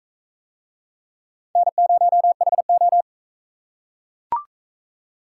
Morse code, heard as a keyed tone, about 44 wpm, 700 Hz. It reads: N0HO